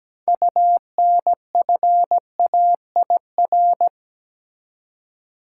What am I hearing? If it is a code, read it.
Morse code, heard as a keyed tone, about 17 wpm, 700 Hz. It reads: UNFAIR
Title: Six Letter Words - 17wpm